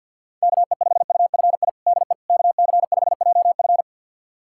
Morse code, heard as a keyed tone, about 51 wpm, 700 Hz. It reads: KE5UFI DE KC5JF
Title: USA Callsign DE Callsign 50wpm